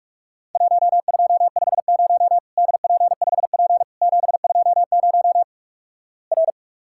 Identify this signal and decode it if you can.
Morse code, heard as a keyed tone, about 46 wpm, 700 Hz. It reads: 1250 DP5P 720 R
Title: Worked All Europe DX Contest - QTC Exchange - 45wpm